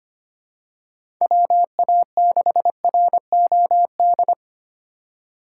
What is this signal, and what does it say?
Morse code, heard as a keyed tone, about 25 wpm, 700 Hz. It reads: WA6ROD
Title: USA Call Signs - 2x Character Spacing - 25wpm